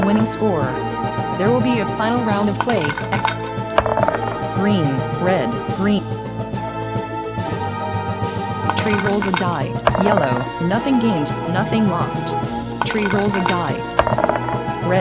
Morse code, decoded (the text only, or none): none